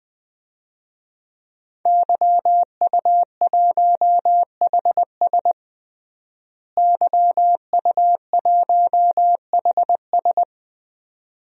Morse code, heard as a keyed tone, about 21 wpm, 700 Hz. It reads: YU1HS YU1HS
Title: DXCC-100 Call Signs - 5 Characters - Repeated 2x - 20wpm